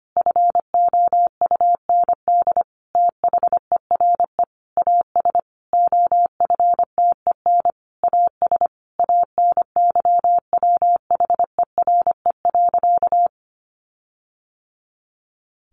Morse code, heard as a keyed tone, about 25 wpm, 700 Hz. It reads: FOUND THERE AS OFTEN AS ANYWHERE.
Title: Sentences from Top 1400 Words 25wpm